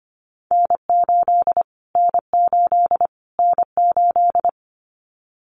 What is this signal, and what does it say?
Morse code, heard as a keyed tone, about 25 wpm, 700 Hz. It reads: N8 N8 N8
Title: Callsign Prefixes for Top 25 DXCC - Repeated 3x - 25wpm